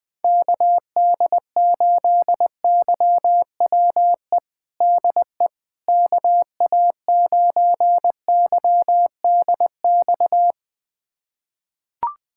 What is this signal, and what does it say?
Morse code, heard as a keyed tone, about 21 wpm, 700 Hz. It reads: KD8YWE DE KA9YDX
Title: USA Callsign DE Callsign 20wpm